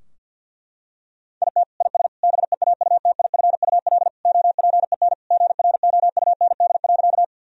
Morse code, heard as a keyed tone, about 50 wpm, 700 Hz. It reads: IT IS BEAUTIFUL OPEN GROUND.
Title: Sentences from Top 1000 Words - Review Part 2 - 50wpm